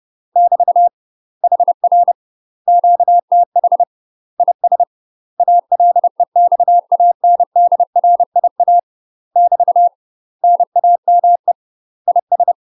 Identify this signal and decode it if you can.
Morse code, heard as a keyed tone, about 30 wpm, 700 Hz. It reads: <BT> HR QTH IS ALEXANDRIA <BT> NAME IS
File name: UK MASTER QSOs TRAINING / 30WPM / UK MASTER QSOs 0050